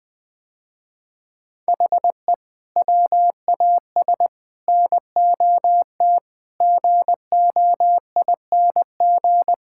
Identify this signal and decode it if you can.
Morse code, heard as a keyed tone, about 20 wpm, 700 Hz. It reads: HE WAS NOT GOING
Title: Sentences with 5-Word Phrases - Spoken Beginning - 20wpm